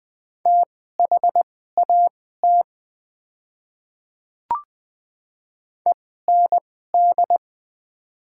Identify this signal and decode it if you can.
Morse code, heard as a keyed tone, about 20 wpm, spaced as for 14 wpm, 700 Hz. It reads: THAT END
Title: 2x Character Spacing - Top 200 Words - 20wpm